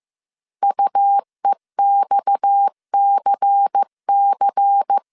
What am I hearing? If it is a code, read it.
Morse code, heard as a keyed tone, about 15 wpm, 800 Hz. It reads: UEXCC
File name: CW / DET / 1 6 1500C S C 14wpm 2951